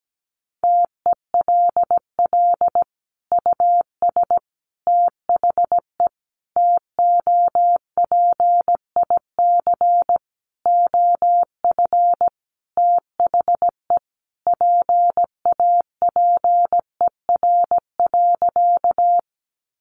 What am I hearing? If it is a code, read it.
Morse code, heard as a keyed tone, about 17 wpm, 700 Hz. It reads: TELL US THE TOPIC OF THE PAPER.
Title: Sentences from Top 1200 Words 17wpm